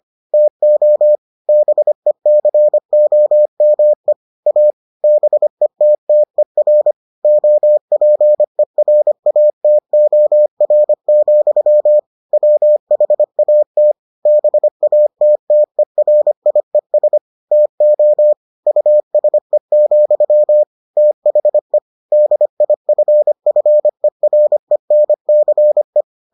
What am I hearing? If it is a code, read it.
Morse code, heard as a keyed tone, about 25 wpm, 600 Hz. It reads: TO BECOME A BETTER OPERATOR, WHAT BATTERIES TO USE, THE DIFFERENCE